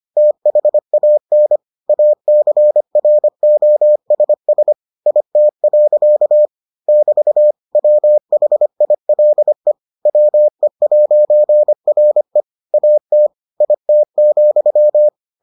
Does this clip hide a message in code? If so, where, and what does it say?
Yes: Morse code from 0.2 to 15.1 s, heard as a keyed tone, about 25 wpm, 600 Hz. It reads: THAN ACROSS IT. <BT> WHILE WE'RE AT IT,